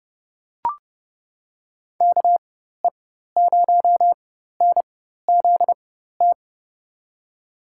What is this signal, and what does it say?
Morse code, heard as a keyed tone, about 30 wpm, spaced as for 14 wpm, 700 Hz. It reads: KE0NZT
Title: USA Call Signs - 6 Characters - 4x Character Spacing - 30wpm